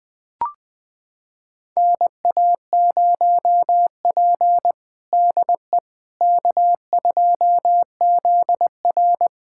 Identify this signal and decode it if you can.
Morse code, heard as a keyed tone, about 21 wpm, 700 Hz. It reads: NA0P DE K2ZR
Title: USA Callsign DE Callsign - 4 Characters - 20wpm